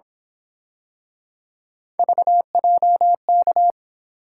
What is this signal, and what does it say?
Morse code, heard as a keyed tone, about 27 wpm, 700 Hz. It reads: VJK
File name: Speed-Racing - Three Letter-Number Combinations - 22wpm